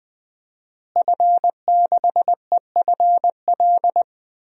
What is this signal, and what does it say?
Morse code, heard as a keyed tone, about 20 wpm, 700 Hz. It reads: F6EFL
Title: DXCC-100 Call Signs - 5 Characters - Repeated 2x - 20wpm